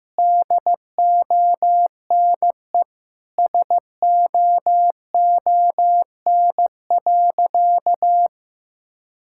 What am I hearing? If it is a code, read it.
Morse code, heard as a keyed tone, about 15 wpm, 700 Hz. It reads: DONE SOON.
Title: Sentences from Top 1600 Words 15wpm